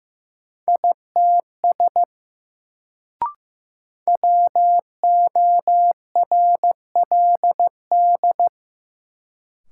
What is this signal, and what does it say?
Morse code, heard as a keyed tone, about 16 wpm, 700 Hz. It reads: ITS WORLD
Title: Sets of 2 Words - New Word Drill - 15wpm